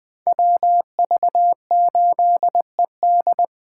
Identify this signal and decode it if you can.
Morse code, heard as a keyed tone, about 20 wpm, 700 Hz. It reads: WV8ED
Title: Speed Racing - USA 2x2 Call Signs 17wpm